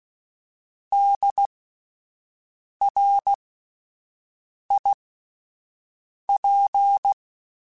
Morse code, heard as a keyed tone, about 16 wpm, spaced as for 5 wpm, 800 Hz. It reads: DRIP